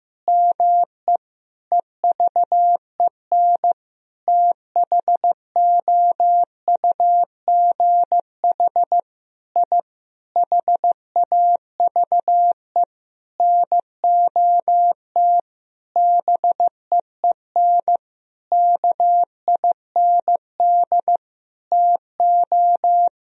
Morse code, heard as a keyed tone, about 15 wpm, 700 Hz. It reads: ME EVEN THOUGH I HAVE NOT BEEN KIND TO
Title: Sentences from Top 1000 Words - Review Part 2 - 15wpm